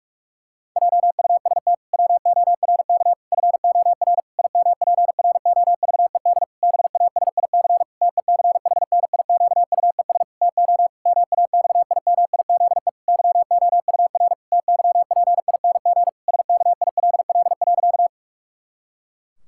Morse code, heard as a keyed tone, about 46 wpm, 700 Hz. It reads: JUST WORK FOR IMPROVED BASIC TECHNIQUES TO MAXIMIZE YOUR TYPING SKILL.
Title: Pangram Challenge v2 - 45wpm